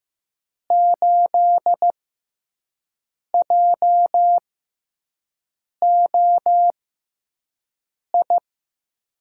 Morse code, heard as a keyed tone, about 15 wpm, spaced as for 5 wpm, 700 Hz. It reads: 8JOI
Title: USA Call Signs - 6 Characters - 6x Character Spacing - 15wpm